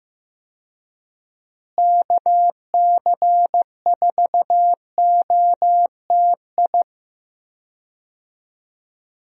Morse code, heard as a keyed tone, about 15 wpm, 700 Hz. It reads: KC4OTI
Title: USA Call Signs - 6 Characters - Repeated 2x - 15wpm